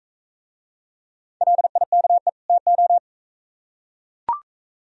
Morse code, heard as a keyed tone, about 43 wpm, 700 Hz. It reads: LIKE TO